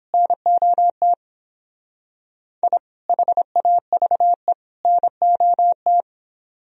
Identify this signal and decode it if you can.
Morse code, heard as a keyed tone, about 31 wpm, 700 Hz. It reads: NOT I HAVE NOT